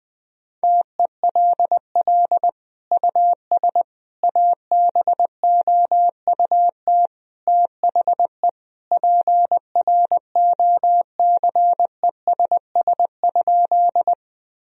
Morse code, heard as a keyed tone, about 21 wpm, 700 Hz. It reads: TELL US ABOUT THE PROCESS?
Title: Sentences from Top 1000 Words - Review Part 3 - 20wpm